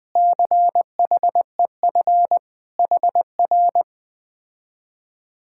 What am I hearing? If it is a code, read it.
Morse code, heard as a keyed tone, about 20 wpm, 700 Hz. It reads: CHEF HR
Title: Speed-Racing Common QSO Phrases Encore - 17wpm